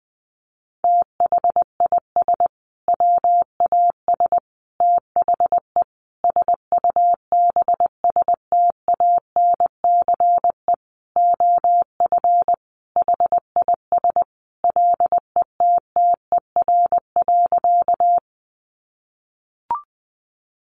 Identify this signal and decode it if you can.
Morse code, heard as a keyed tone, about 20 wpm, 700 Hz. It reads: THIS WAS THE SUBSTANCE OF HIS LETTER.
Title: Sentences from Top 1500 Words 20wpm